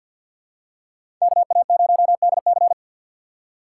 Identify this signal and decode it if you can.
Morse code, heard as a keyed tone, about 53 wpm, 700 Hz. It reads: KA0DC
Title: USA Call Signs - 5 Characters - 3x Character Spacing - 50wpm